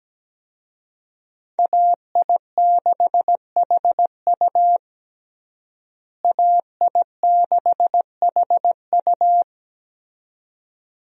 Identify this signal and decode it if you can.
Morse code, heard as a keyed tone, about 17 wpm, 700 Hz. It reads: AI6HU AI6HU
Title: USA Call Signs - 5 Characters - Repeated 2x - 17wpm